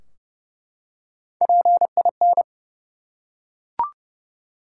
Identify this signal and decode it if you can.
Morse code, heard as a keyed tone, about 30 wpm, 700 Hz. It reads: PIN
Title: Three Letter Words - 30wpm